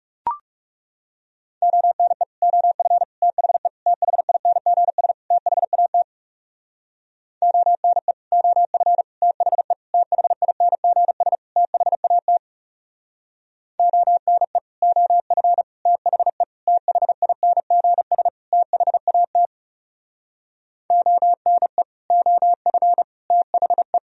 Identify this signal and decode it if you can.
Morse code, heard as a keyed tone, about 39 wpm, 700 Hz. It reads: ONE OF THE THINGS THAT ONE OF THE THINGS THAT ONE OF THE THINGS THAT ONE OF THE